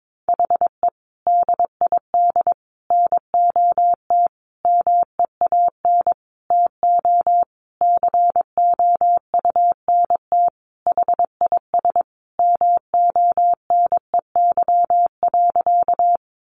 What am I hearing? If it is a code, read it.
Morse code, heard as a keyed tone, about 22 wpm, 700 Hz. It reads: HE DID NOT MEAN TO COUNT HIS MONEY.
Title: Sentences from Top 1500 Words 22wpm